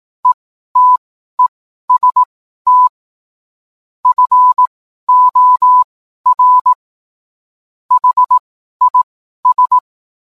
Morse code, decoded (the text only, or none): ETEST FOR HIS